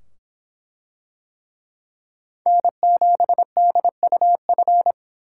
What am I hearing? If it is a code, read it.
Morse code, heard as a keyed tone, about 24 wpm, 700 Hz. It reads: N7DUF